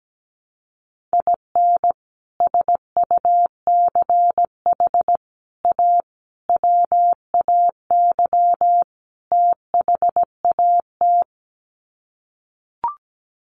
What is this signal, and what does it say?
Morse code, heard as a keyed tone, about 17 wpm, 700 Hz. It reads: IN SUCH A WAY THAT
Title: Sets of 5 Words 17wpm